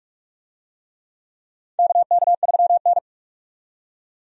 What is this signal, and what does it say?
Morse code, heard as a keyed tone, about 46 wpm, 700 Hz. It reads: KK3N
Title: USA Call Signs - 4 Characters - 45wpm